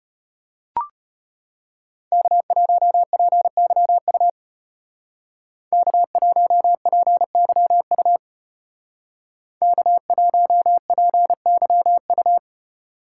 Morse code, heard as a keyed tone, about 30 wpm, 700 Hz. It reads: K1PYU K1PYU K1PYU
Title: Speed Racing - USA 1x3 Call Signs 25wpm